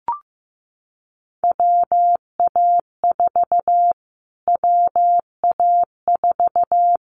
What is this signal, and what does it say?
Morse code, heard as a keyed tone, about 15 wpm, 700 Hz. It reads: WA4 WA4
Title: Callsign Prefixes for Top 25 DXCC - Repeated 3x - 15wpm